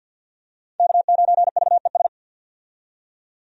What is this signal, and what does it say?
Morse code, heard as a keyed tone, about 51 wpm, 700 Hz. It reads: K9VES
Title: USA Call Signs - 5 Characters - 50wpm